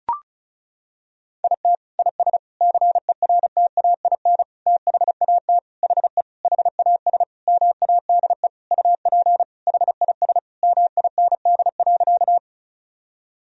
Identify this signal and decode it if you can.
Morse code, heard as a keyed tone, about 34 wpm, 700 Hz. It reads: IT IS CERTAIN THAT HE HAS MADE UP HIS MIND.
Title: Sentences from Top 1000 Words - Review Part 2 - 35wpm